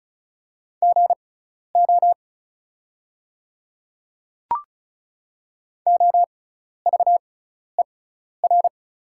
Morse code, heard as a keyed tone, about 36 wpm, spaced as for 12 wpm, 700 Hz. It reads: GO OVER
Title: 6x Character Spacing - Top 100 Words - 35wpm